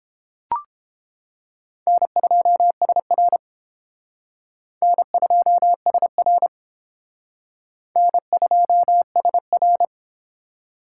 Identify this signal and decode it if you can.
Morse code, heard as a keyed tone, about 29 wpm, 700 Hz. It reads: N2SR N2SR N2SR